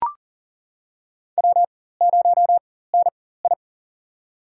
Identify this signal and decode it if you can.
Morse code, heard as a keyed tone, about 40 wpm, spaced as for 19 wpm, 700 Hz. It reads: W0NI
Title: USA Call Signs - 4 Characters - 4x Character Spacing - 40wpm